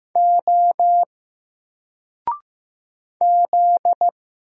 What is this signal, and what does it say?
Morse code, heard as a keyed tone, about 15 wpm, 700 Hz. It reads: O Z
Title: Single Letter-Number 15wpm